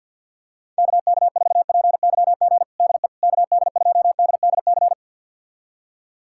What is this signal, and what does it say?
Morse code, heard as a keyed tone, about 50 wpm, 700 Hz. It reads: KK4PYG DE KD2DDC
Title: USA Callsign DE Callsign 50wpm